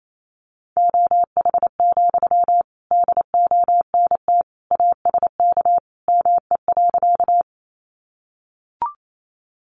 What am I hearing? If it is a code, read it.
Morse code, heard as a keyed tone, about 28 wpm, 700 Hz. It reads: OH, DONT ASK ME.
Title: Sentences from Top 500 Words 28wpm